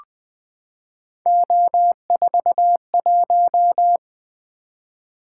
Morse code, heard as a keyed tone, about 20 wpm, 700 Hz. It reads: O41
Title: Three Letter-Number Combinations - 20wpm